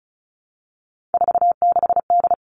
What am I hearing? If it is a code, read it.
Morse code, heard as a keyed tone, about 36 wpm, 700 Hz. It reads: 46D
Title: Three Challenging Letter-Number Combinations 35wpm